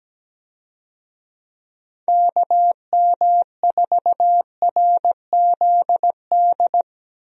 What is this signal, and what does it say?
Morse code, heard as a keyed tone, about 17 wpm, 700 Hz. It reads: KM4RZD
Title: USA Call Signs - 6 Characters - Repeated 2x - 17wpm